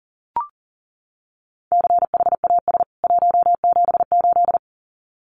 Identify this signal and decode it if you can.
Morse code, heard as a keyed tone, about 40 wpm, 700 Hz. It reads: CHAS 178
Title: CWT Exchange 40wpm